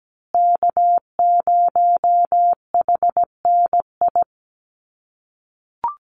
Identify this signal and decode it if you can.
Morse code, heard as a keyed tone, about 17 wpm, 700 Hz. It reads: K0HNI
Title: USA Call Signs Repeated 3x 17wpm